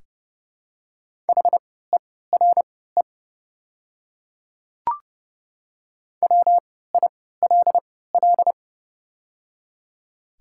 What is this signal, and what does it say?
Morse code, heard as a keyed tone, about 30 wpm, spaced as for 17 wpm, 700 Hz. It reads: HERE WILL
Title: 3x Character Spacing - Top 500 Words - 30wpm